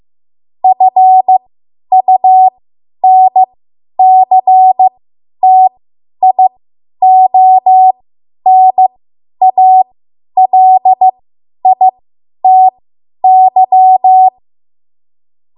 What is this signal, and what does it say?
Morse code, heard as a keyed tone, about 15 wpm, spaced as for 10 wpm, 750 Hz. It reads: FUNCTIONALITY
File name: Morse / 10WPM